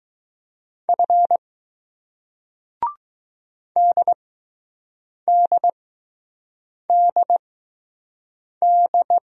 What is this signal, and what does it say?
Morse code, heard as a keyed tone, about 20 wpm, 700 Hz. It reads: F D D D D